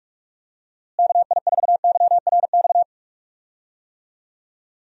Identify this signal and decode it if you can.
Morse code, heard as a keyed tone, about 45 wpm, 700 Hz. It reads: KI4YRX
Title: USA Call Signs - 6 Characters - Repeated 2x - 45wpm